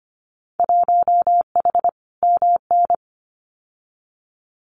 Morse code, heard as a keyed tone, about 25 wpm, 700 Hz. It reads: 1H MN
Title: Winter Field Day Exchanges Encore 25wpm